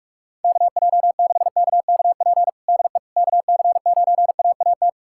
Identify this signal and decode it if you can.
Morse code, heard as a keyed tone, about 42 wpm, 700 Hz. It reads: KJ6KKP DE KC9AAT
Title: USA Callsign DE Callsign - 6 Characters - 45wpm